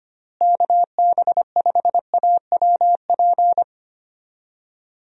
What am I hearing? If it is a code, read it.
Morse code, heard as a keyed tone, about 25 wpm, 700 Hz. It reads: KB5AWP
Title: USA Call Signs - 6 Characters - Repeated 2x - 25wpm